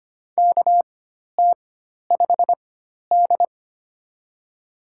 Morse code, heard as a keyed tone, about 25 wpm, spaced as for 12 wpm, 690 Hz. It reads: KT5D
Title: USA Call Signs - 4 Characters - 4x Character Spacing - 25wpm